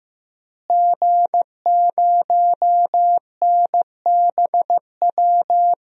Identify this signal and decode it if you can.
Morse code, heard as a keyed tone, about 15 wpm, 700 Hz. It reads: G0NBW
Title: DXCC-100 Call Signs - 5 Characters - Repeated 3x - 15wpm